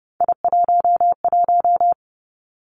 Morse code, heard as a keyed tone, about 30 wpm, 700 Hz. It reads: I11
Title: Three Challenging Letter-Number Combinations 30wpm